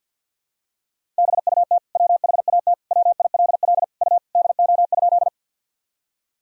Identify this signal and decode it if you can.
Morse code, heard as a keyed tone, about 51 wpm, 700 Hz. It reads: BUT WHAT WILL U DO?
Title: Sentences from Top 100 Words and Abbreviations 50wpm